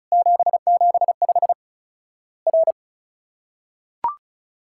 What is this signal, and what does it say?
Morse code, heard as a keyed tone, about 36 wpm, 700 Hz. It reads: 775 R
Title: Worked All Europe DX Contest - QTC Exchange - 35wpm